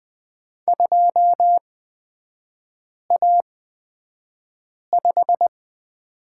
Morse code, heard as a keyed tone, about 20 wpm, spaced as for 5 wpm, 700 Hz. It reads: 2A5
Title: 236 Lesson 38e - Letter q - Three Characters 20wpm